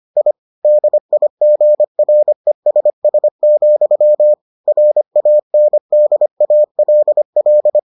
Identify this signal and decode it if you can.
Morse code, heard as a keyed tone, about 25 wpm, 600 Hz. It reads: I DIGRESS, RANDALL